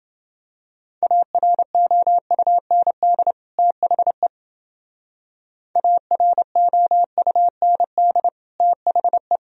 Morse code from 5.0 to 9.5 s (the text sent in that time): AROUND THE